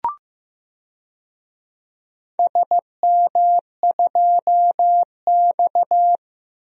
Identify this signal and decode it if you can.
Morse code, heard as a keyed tone, about 16 wpm, 700 Hz. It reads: SM2X
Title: DXCC-100 Call Signs - 4 Characters - 15wpm